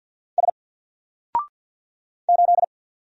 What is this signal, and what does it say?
Morse code, heard as a keyed tone, about 48 wpm, 700 Hz. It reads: S 8
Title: Single Character (Letter:Number:Punctuation:Prosign) 50wpm